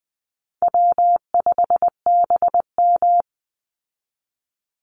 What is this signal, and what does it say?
Morse code, heard as a keyed tone, about 20 wpm, 700 Hz. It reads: W5BM
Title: Easy Call Signs Repeated 3x - 20wpm